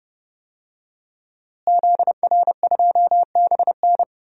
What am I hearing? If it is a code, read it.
Morse code, heard as a keyed tone, about 30 wpm, 700 Hz. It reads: ZR2BN
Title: DXCC-100 Call Signs - 5 Characters - Repeated 3x - 30wpm